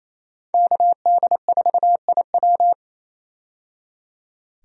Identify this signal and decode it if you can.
Morse code, heard as a keyed tone, about 28 wpm, 700 Hz. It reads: KD4IW
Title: USA Call Signs - 5 Characters - 28wpm